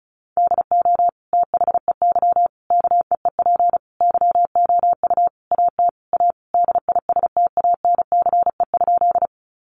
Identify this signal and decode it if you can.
Morse code, heard as a keyed tone, about 34 wpm, 700 Hz. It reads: DO THEY KEEP YOU AT A DISTANCE?
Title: Sentences from Top 1200 Words 35wpm